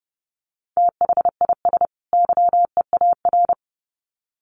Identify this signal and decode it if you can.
Morse code, heard as a keyed tone, about 30 wpm, 700 Hz. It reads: THIS YEAR
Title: Sets of 2 Words 30wpm